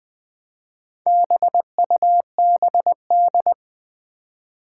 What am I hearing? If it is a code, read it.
Morse code, heard as a keyed tone, about 20 wpm, 700 Hz. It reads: BUBD
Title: Four Challenging Letter-Number Combinations 20wpm